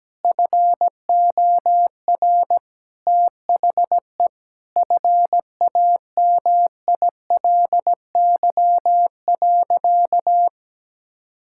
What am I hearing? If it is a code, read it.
Morse code, heard as a keyed tone, about 17 wpm, 700 Hz. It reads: FOR THE FAMILY.
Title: Sentences from Top 1600 Words 17wpm